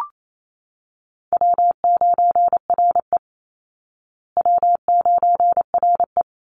Morse code, heard as a keyed tone, about 28 wpm, 700 Hz. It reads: W9RE W9RE
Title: Easy Call Signs Repeated 2x - 28wpm